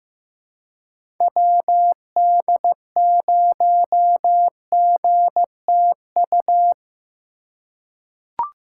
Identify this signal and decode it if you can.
Morse code, heard as a keyed tone, about 15 wpm, 700 Hz. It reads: WD0GTU
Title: USA Call Signs - 6 Characters - 2x Character Spacing - 15wpm